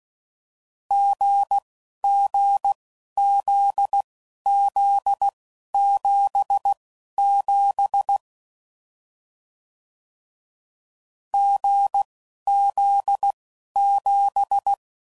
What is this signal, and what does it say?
Morse code, heard as a keyed tone, about 16 wpm, spaced as for 11 wpm, 800 Hz. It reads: GGZZ77 GZ7
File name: Combinations / Difficult Combos / GZ GZ7